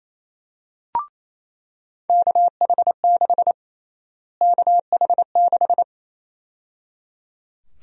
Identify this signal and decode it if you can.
Morse code, heard as a keyed tone, about 28 wpm, 700 Hz. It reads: KH6 KH6
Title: USA Call Sign Prefixes - Repeated 2x - 28wpm